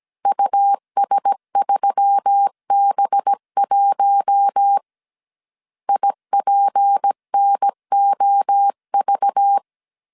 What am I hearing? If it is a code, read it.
Morse code, heard as a keyed tone, about 17 wpm, 800 Hz. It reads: US3B1 IPNOV
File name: 16wpm